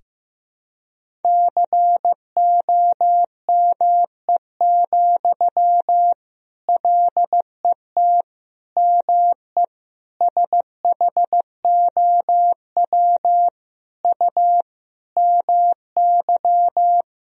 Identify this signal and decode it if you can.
Morse code, heard as a keyed tone, about 15 wpm, 700 Hz. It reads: COME, LET ME SHOW U MY
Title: Sentences from Top 500 Words and Abbreviations 15wpm